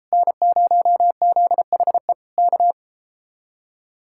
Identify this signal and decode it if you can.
Morse code, heard as a keyed tone, about 33 wpm, 700 Hz. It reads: N0ZHE K